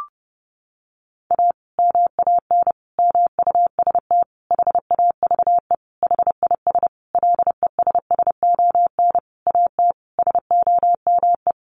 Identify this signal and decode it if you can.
Morse code, heard as a keyed tone, about 30 wpm, 700 Hz. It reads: A MAN MUST HAVE HIS LESSON AT SOME
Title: Sentences from Top 1200 Words 30wpm